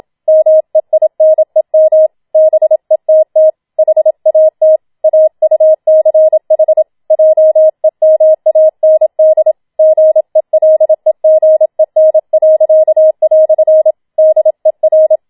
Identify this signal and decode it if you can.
Morse code, heard as a keyed tone, about 27 wpm, 630 Hz. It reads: MEINEM BETT HAT AUCH JEMAND GELEGEN." DER